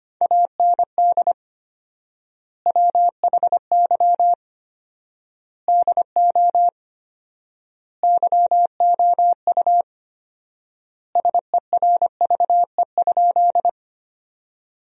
Morse code, heard as a keyed tone, about 25 wpm, 700 Hz. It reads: AND WHY DO YOU SERVE?
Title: 4x Word-Spacing - Sentences from Top 400 Words - 25wpm